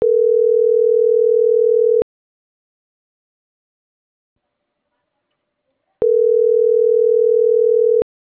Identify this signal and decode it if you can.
Ringback tone (North America), 440 + 480 Hz.